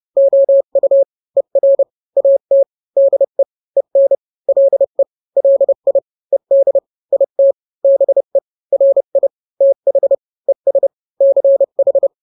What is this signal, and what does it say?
Morse code, heard as a keyed tone, about 30 wpm, 550 Hz. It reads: OU ER AT DE EN LE LI ED IT BE RI TH ES CH
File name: n-grams / 2-grams 30 25 WPM